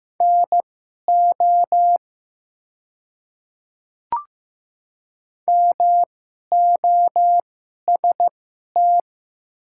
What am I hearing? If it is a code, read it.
Morse code, heard as a keyed tone, about 15 wpm, spaced as for 11 wpm, 690 Hz. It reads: NO MOST
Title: 2x Character Spacing - Top 100 Words - 15wpm